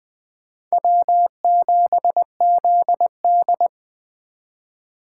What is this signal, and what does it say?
Morse code, heard as a keyed tone, about 21 wpm, 700 Hz. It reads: W7ZD
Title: Speed Racing - USA Call Signs 17wpm